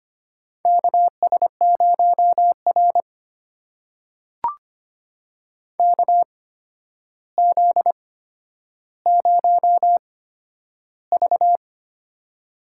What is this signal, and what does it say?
Morse code, heard as a keyed tone, about 25 wpm, spaced as for 11 wpm, 700 Hz. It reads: KS0R KZ0V
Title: USA Call Signs - 4 Characters - 8x Character Spacing - 25wpm